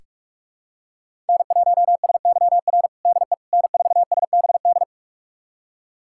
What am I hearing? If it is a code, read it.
Morse code, heard as a keyed tone, about 42 wpm, 700 Hz. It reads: N1SYR DE N4SBD
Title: USA Callsign DE Callsign - 5 Characters - 45wpm